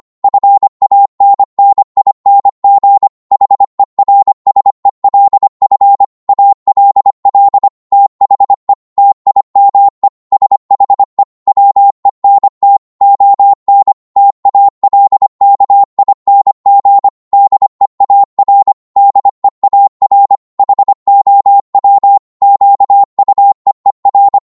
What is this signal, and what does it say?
Morse code, heard as a keyed tone, about 25 wpm, 800 Hz. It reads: FANNING HERSELF ALL THE TIME SHE WENT ON TALKING DEAR DEAR HOW QUEER